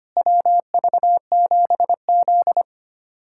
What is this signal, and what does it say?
Morse code, heard as a keyed tone, about 25 wpm, 700 Hz. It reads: WV7Z
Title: USA Callsign DE Callsign - 4 Characters - 25wpm